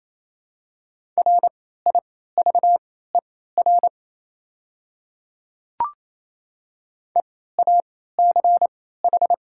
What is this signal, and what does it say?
Morse code, heard as a keyed tone, about 27 wpm, spaced as for 16 wpm, 700 Hz. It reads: RIVER EACH
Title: 3x Character Spacing - Top 500 Words - 28wpm